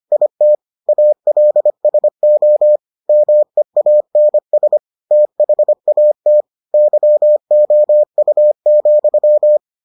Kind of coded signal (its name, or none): Morse code